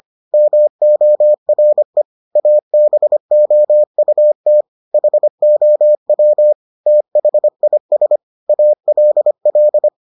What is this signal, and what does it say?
Morse code, heard as a keyed tone, about 25 wpm, 600 Hz. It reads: MORE ABOUT HOW THIS ALL